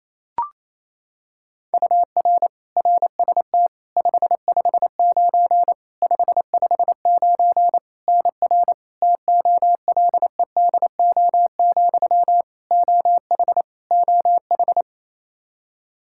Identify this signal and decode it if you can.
Morse code, heard as a keyed tone, about 27 wpm, 700 Hz. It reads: UR RST 559 559 NR TOLEDO, OH OH
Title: Practice QSO - Exchange 1 - Beginning - 28wpm